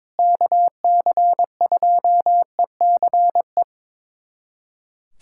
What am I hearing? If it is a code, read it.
Morse code, heard as a keyed tone, about 22 wpm, 700 Hz. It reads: KC2ECE
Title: USA Callsign DE Callsign 22wpm